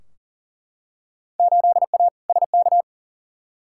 Morse code, heard as a keyed tone, about 40 wpm, 700 Hz. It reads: 8A SK
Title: Field Day Exchanges Encore - 40wpm